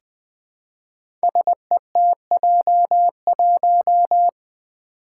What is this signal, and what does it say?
Morse code, heard as a keyed tone, about 20 wpm, 700 Hz. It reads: SETJ1
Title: Five Challenging Letter-Number Combinations 20wpm